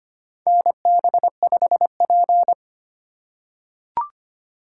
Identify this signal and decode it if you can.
Morse code, heard as a keyed tone, about 25 wpm, 700 Hz. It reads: NB5P
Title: USA Call Signs - 4 Characters - 2x Character Spacing - 25wpm